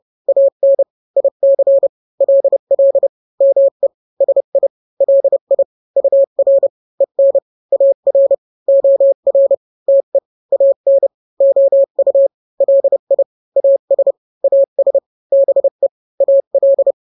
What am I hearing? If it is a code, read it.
Morse code, heard as a keyed tone, about 30 wpm, 550 Hz. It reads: AN IC LL ME SI LI UR EN AR OR TE AN OU LI AS AS BE AL
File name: n-grams / 2-grams 30 25 WPM